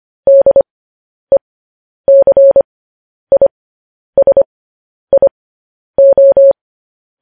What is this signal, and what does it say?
Morse code, heard as a keyed tone, about 25 wpm, spaced as for 10 wpm, 550 Hz. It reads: DECISIO
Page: news headlines in morse code at 10 wpm